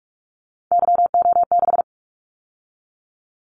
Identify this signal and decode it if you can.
Morse code, heard as a keyed tone, about 44 wpm, 700 Hz. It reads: YO6
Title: Callsign Prefixes for Top 25 DXCC - 45wpm